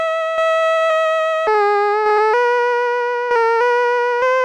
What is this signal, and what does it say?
There is no code.